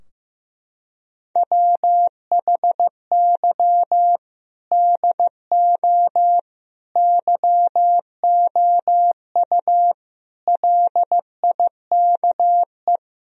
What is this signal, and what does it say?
Morse code, heard as a keyed tone, about 15 wpm, 700 Hz. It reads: WHY DO YOU LIKE
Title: Sentences from Top 1000 Words - Review Part 4 - 15wpm